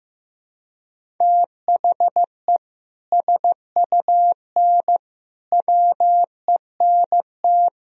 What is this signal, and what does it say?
Morse code, heard as a keyed tone, about 15 wpm, 700 Hz. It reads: THE SUN WENT
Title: Sentences from Top 1000 Words - Review Part 1 - 15wpm